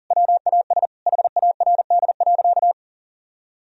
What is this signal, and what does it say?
Morse code, heard as a keyed tone, about 40 wpm, 700 Hz. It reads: WAS HARD.
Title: Sentences from Top 1000 Words - Review Part 4 - 40wpm